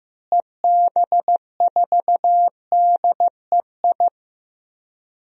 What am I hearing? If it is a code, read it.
Morse code, heard as a keyed tone, about 16 wpm, 700 Hz. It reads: EB4DEI
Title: DXCC-100 Call Signs - 6 Characters - Repeated 3x - 15wpm